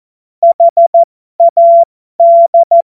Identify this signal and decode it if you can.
Morse code, heard as a keyed tone, about 14 wpm, spaced as for 12 wpm, 680 Hz. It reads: HAD
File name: Morsekurs / Witze